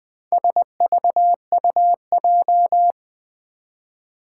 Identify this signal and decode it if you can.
Morse code, heard as a keyed tone, about 20 wpm, 700 Hz. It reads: SVUJ
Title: Four Challenging Letter-Number Combinations 20wpm